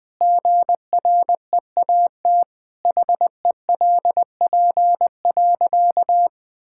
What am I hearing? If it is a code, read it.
Morse code, heard as a keyed tone, about 21 wpm, 700 Hz. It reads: GREAT HELP.
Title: Sentences with 3-Word Phrases - 20wpm